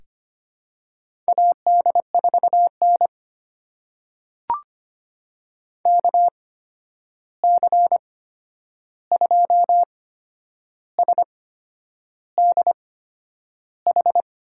Morse code, heard as a keyed tone, about 25 wpm, spaced as for 9 wpm, 700 Hz. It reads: AD4N KC2SDH
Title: USA Call Signs - 8x Character Spacing - 25wpm